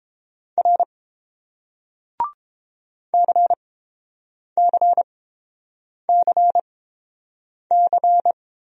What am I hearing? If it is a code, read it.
Morse code, heard as a keyed tone, about 29 wpm, spaced as for 7 wpm, 700 Hz. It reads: R CCCC